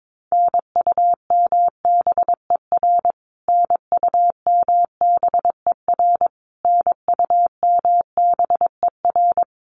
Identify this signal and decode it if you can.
Morse code, heard as a keyed tone, about 22 wpm, 700 Hz. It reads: NUMBER NUMBER NUMBER
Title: Top 300 Words Repeated 22wpm